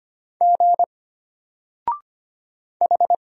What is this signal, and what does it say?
Morse code, heard as a keyed tone, about 25 wpm, 700 Hz. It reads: G H
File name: Single-Letter-Number-25wpm